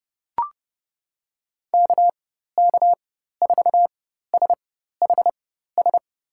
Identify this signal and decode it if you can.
Morse code, heard as a keyed tone, about 30 wpm, spaced as for 14 wpm, 700 Hz. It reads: KK4SHS
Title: USA Call Signs - 6 Characters - 4x Character Spacing - 30wpm